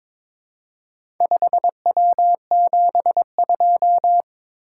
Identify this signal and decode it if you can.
Morse code, heard as a keyed tone, about 22 wpm, 700 Hz. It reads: 5W72
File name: practice / 22wpm / Four Letter-Number Combinations - 22wpm